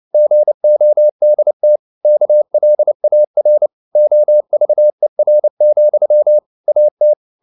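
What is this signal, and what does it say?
Morse code, heard as a keyed tone, about 29 wpm, 600 Hz. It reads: GODT KLAR OVER, AT
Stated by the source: Fast (QRQ) Morse at 29 wpm, from CW med Gnister.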